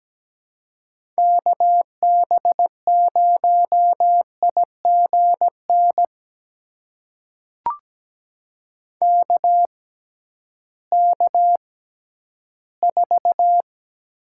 Morse code, heard as a keyed tone, about 17 wpm, spaced as for 11 wpm, 700 Hz. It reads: KB0IGN KK4
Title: USA Call Signs - 6 Characters - 6x Character Spacing - 17wpm